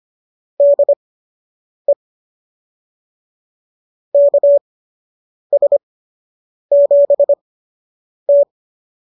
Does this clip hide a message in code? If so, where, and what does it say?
Yes: Morse code from 0.6 to 8.4 s, heard as a keyed tone, about 25 wpm, spaced as for 8 wpm, 590 Hz. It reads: DE KS7T